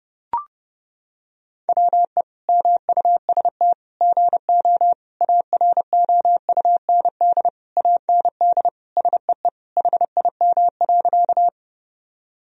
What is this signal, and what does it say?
Morse code, heard as a keyed tone, about 30 wpm, 700 Hz. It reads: WE MUST GO AROUND AND SEE HIM.
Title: Sentences from Top 1000 Words - Review Part 3 - 30wpm